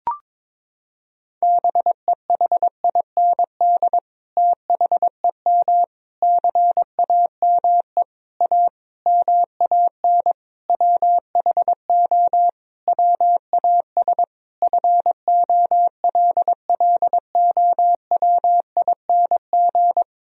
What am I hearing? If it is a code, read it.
Morse code, heard as a keyed tone, about 22 wpm, 700 Hz. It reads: BEHIND THEM CAME A MAN WHO WAS FOLLOWING
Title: Sentences from Top 1000 Words - Review Part 4 - 22wpm